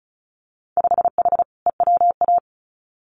Morse code, heard as a keyed tone, about 34 wpm, 700 Hz. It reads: 5H EWA
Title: Winter Field Day Exchanges Encore 35wpm